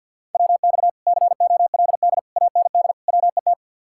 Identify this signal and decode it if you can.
Morse code, heard as a keyed tone, about 67 wpm, spaced as for 58 wpm, 700 Hz. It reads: WX COLD AND WET
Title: Speed-Racing Common QSO Phrases - 50wpm